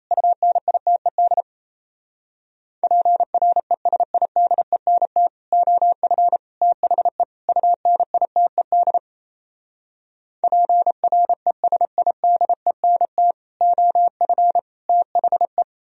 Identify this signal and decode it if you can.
Morse code, heard as a keyed tone, about 38 wpm, 690 Hz. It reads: UNITED PRESIDENT OF THE UNITED PRESIDENT OF THE